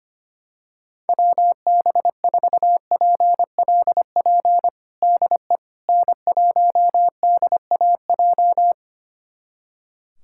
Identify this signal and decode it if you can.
Morse code, heard as a keyed tone, about 25 wpm, 700 Hz. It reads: WB4PLP DE N1DAJ
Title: USA Callsign DE Callsign 25wpm